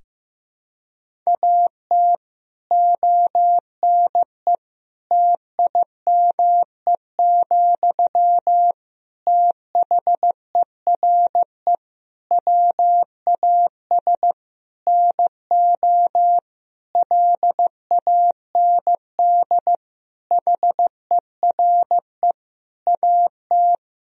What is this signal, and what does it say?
Morse code, heard as a keyed tone, about 15 wpm, 700 Hz. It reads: AT ONE TIME, THERE WAS NO LAND HERE AT
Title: Sentences from Top 1000 Words - Review Part 1 - 15wpm